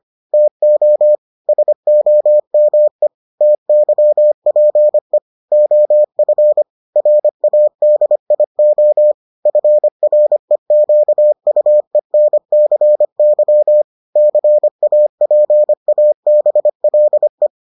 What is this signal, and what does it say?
Morse code, heard as a keyed tone, about 25 wpm, 600 Hz. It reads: TO SOME TYPE OF RADIO FREQUENCY CAPABLE